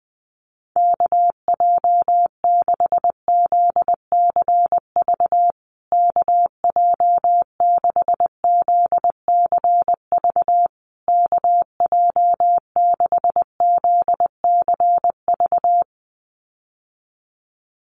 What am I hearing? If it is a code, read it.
Morse code, heard as a keyed tone, about 20 wpm, 700 Hz. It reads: KJ6ZCV KJ6ZCV KJ6ZCV
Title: USA Call Signs Repeated 3x 20wpm